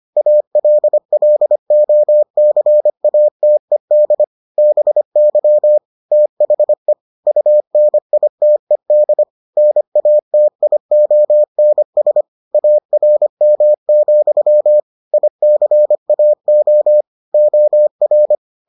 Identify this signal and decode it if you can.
Morse code, heard as a keyed tone, about 25 wpm, 600 Hz. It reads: ALLOCATED BY THE UNITED NATIONS ARM, ICAO OR